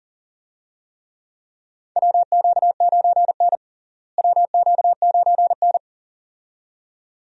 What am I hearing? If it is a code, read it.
Morse code, heard as a keyed tone, about 40 wpm, 700 Hz. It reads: WQ9N WQ9N
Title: USA Call Signs - 4 Characters - Repeated 2x - 40wpm